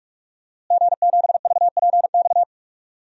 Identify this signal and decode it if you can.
Morse code, heard as a keyed tone, about 41 wpm, 700 Hz. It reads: G7VPX
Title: DXCC-100 Call Signs - 5 Characters - Repeated 3x - 45wpm